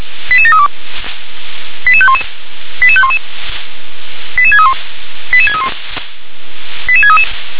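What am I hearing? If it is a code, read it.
Five-tone selective call (ZVEI), heard as quick runs of five tones, tones of about 70 ms. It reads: 88732 88511 88511 88631 88631 88633
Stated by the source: ZVEI five-tone selective call 88510 received on an SDR.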